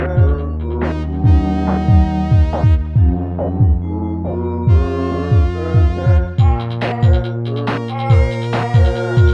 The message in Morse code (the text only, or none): none